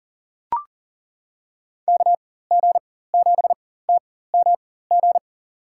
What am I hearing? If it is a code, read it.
Morse code, heard as a keyed tone, about 40 wpm, spaced as for 19 wpm, 700 Hz. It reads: KG7TMG
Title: USA Call Signs - 6 Characters - 4x Character Spacing - 40wpm